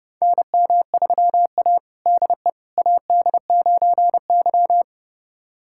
Morse code, heard as a keyed tone, about 32 wpm, 700 Hz. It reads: NM3A DE AD9Y
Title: USA Callsign DE Callsign - 4 Characters - 30wpm